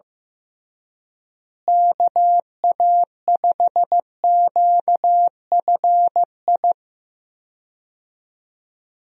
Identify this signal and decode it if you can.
Morse code, heard as a keyed tone, about 15 wpm, 700 Hz. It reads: KA5QFI
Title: USA Call Signs - 6 Characters - Repeated 3x - 15wpm